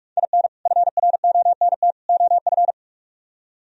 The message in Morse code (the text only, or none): IN FRONT OF